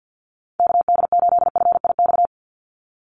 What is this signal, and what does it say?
Morse code, heard as a keyed tone, about 50 wpm, 700 Hz. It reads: KD8FIX
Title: USA Call Signs 50wpm